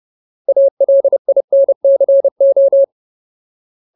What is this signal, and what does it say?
Morse code, heard as a keyed tone, about 30 wpm, 550 Hz. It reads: ALINCO
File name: Ultimate Word List / 30WPM Ultimate